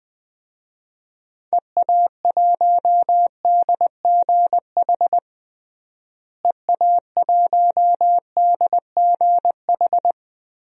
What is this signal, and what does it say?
Morse code, heard as a keyed tone, about 21 wpm, 700 Hz. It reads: EA1DGH EA1DGH
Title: DXCC-100 Call Signs - 6 Characters - Repeated 3x - 20wpm